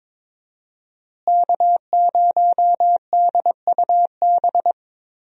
Morse code, heard as a keyed tone, about 22 wpm, 700 Hz. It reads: K0DUB
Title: USA Call Signs - 5 Characters - Repeated 2x - 22wpm